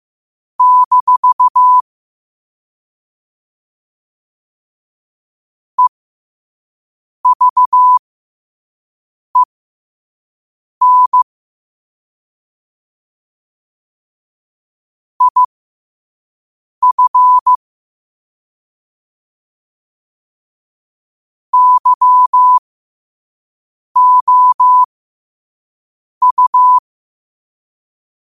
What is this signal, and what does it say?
Morse code, heard as a keyed tone, about 15 wpm, spaced as for 5 wpm, 1000 Hz. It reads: - EVEN IF YOU
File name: qotdfiles / 5wpm